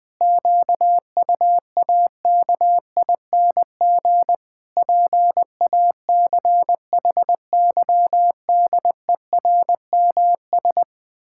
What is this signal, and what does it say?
Morse code, heard as a keyed tone, about 21 wpm, 700 Hz. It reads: QUAKING PACHYDERMS
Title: Pangram Challenge v2 - 20wpm